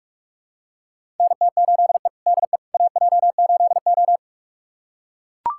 Morse code, heard as a keyed tone, about 43 wpm, 700 Hz. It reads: NT8E DE AJ8O
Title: USA Callsign DE Callsign - 4 Characters - 45wpm